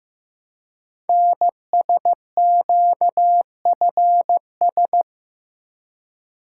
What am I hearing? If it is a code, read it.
Morse code, heard as a keyed tone, about 15 wpm, 700 Hz. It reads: NSQFS
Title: Code Groups - 5 Letters - 15wpm